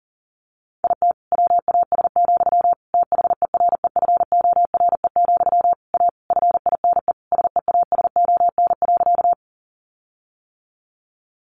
Morse code, heard as a keyed tone, about 40 wpm, 700 Hz. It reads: IT WAS, THEREFORE, A FINE SEASON.
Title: Sentences from Top 1400 Words 40wpm